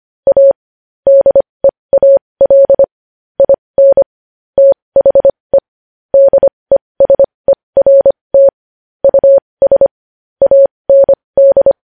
Morse code, heard as a keyed tone, about 25 wpm, spaced as for 20 wpm, 550 Hz. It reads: A DEAL IN THE DESERT US AND
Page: news-headlines-in-morse-code-at-20-wpm